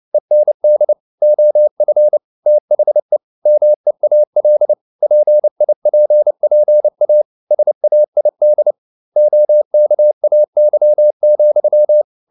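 Morse code, heard as a keyed tone, about 29 wpm, 600 Hz. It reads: END OF THE MEAL PIPPA SAID OKAY,
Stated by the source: Morse code at 29 wpm, sending Ken Follett's novel Never in English, from CW med Gnister.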